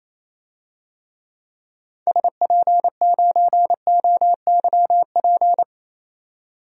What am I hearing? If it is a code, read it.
Morse code, heard as a keyed tone, about 28 wpm, 700 Hz. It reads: SP9OYP
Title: DXCC-100 Call Signs - 6 Characters - 28wpm